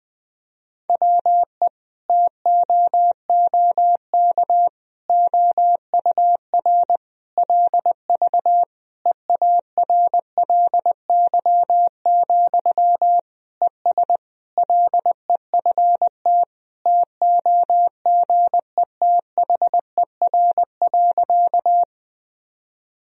Morse code, heard as a keyed tone, about 21 wpm, 700 Hz. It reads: WE TOOK OUR LV EARLY, ES LEFT TOGETHER.
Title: Sentences from Top 400 Words and Abbreviations 20wpm